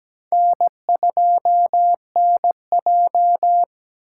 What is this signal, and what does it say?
Morse code, heard as a keyed tone, about 20 wpm, 700 Hz. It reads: N2NJ
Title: Speed Racing - USA 1x2 Call Signs 17wpm